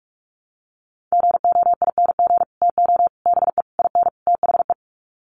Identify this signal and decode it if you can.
Morse code, heard as a keyed tone, about 44 wpm, 700 Hz. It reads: GOING TO BE IN THE
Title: Sets of 5 Words Encore 45wpm